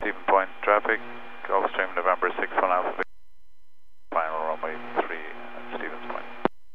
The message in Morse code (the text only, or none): none